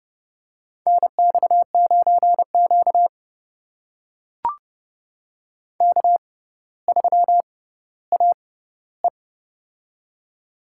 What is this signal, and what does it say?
Morse code, heard as a keyed tone, about 32 wpm, spaced as for 16 wpm, 700 Hz. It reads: NX9Q K3AE
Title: USA Call Signs - 4 Characters - 6x Character Spacing - 30wpm